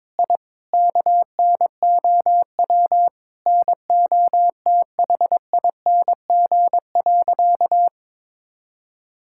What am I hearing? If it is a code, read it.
Morse code, heard as a keyed tone, about 21 wpm, 700 Hz. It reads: I KNOW NOTHING.
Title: Sentences from Top 1000 Words - Review Part 6 - 22wpm